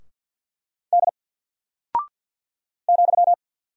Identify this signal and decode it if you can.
Morse code, heard as a keyed tone, about 48 wpm, 700 Hz. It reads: D ,
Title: Single Character (Letter:Number:Punctuation:Prosign) Alt 50wpm